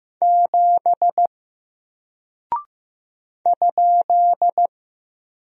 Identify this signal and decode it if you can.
Morse code, heard as a keyed tone, about 15 wpm, 700 Hz. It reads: 7 ?
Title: Single Character (Letter:Number:Punctuation:Prosign) Alt 15wpm